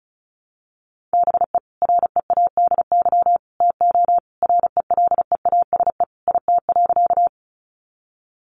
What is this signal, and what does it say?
Morse code, heard as a keyed tone, about 34 wpm, 700 Hz. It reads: BE READY TO RELEASE IT.
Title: Sentences from Top 1100 Words 35wpm